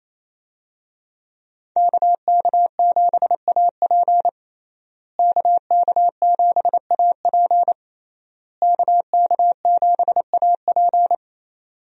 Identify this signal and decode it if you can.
Morse code, heard as a keyed tone, about 28 wpm, 700 Hz. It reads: KK7AP KK7AP KK7AP
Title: USA Call Signs - 5 Characters - Repeated 3x - 28wpm